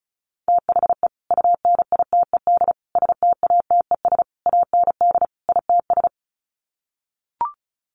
Morse code, heard as a keyed tone, about 34 wpm, 700 Hz. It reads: THE UNITED STATES AND ITS
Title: Sets of 5 Words Encore 35wpm